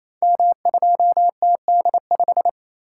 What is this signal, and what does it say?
Morse code, heard as a keyed tone, about 27 wpm, 700 Hz. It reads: M2TD5
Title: Code Groups - 5 Letter:Numbers - 28wpm